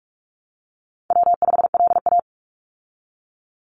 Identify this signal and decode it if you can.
Morse code, heard as a keyed tone, about 44 wpm, 700 Hz. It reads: W5LA
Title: Call Signs Repeated 3x - 45wpm